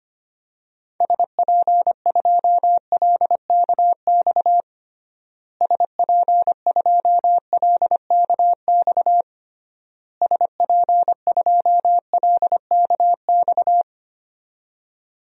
Morse code, heard as a keyed tone, about 25 wpm, 700 Hz. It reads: SP2LKX SP2LKX SP2LKX
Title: DXCC-100 Call Signs - 6 Characters - Repeated 3x - 25wpm